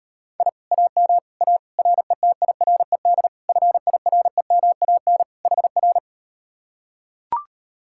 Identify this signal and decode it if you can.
Morse code, heard as a keyed tone, about 38 wpm, 690 Hz. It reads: I AM A RETIRED FIREMAN HR